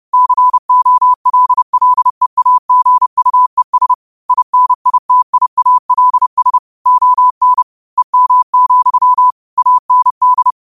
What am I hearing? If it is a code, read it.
Morse code, heard as a keyed tone, about 30 wpm, 1000 Hz. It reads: COLLEAGUES INITIALS ON EM, AND
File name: qotdfiles / 30wpm